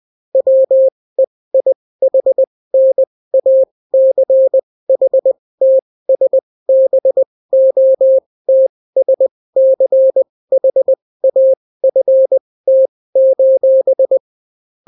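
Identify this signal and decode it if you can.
Morse code, heard as a keyed tone, about 20 wpm, spaced as for 16 wpm, 530 Hz. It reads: WEIHNACHTSBOTSCHAFT:
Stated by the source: Daily Morse weather and top stories at 16 wpm, clean without QRN.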